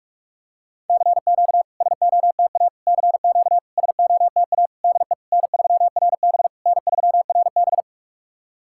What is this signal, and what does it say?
Morse code, heard as a keyed tone, about 44 wpm, 690 Hz. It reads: CQ SOTA CQ SOTA DE N3RB N3RB
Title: SOTA QSO Practice 45wpm